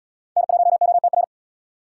Morse code, heard as a keyed tone, about 68 wpm, 700 Hz. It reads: N1OTR